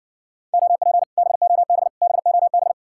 Morse code, heard as a keyed tone, about 60 wpm, 700 Hz. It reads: OP BOB BOB